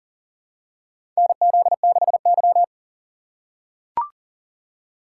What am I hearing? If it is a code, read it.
Morse code, heard as a keyed tone, about 40 wpm, 700 Hz. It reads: NZ6Y
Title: USA Call Signs - 4 Characters - 3x Character Spacing - 40wpm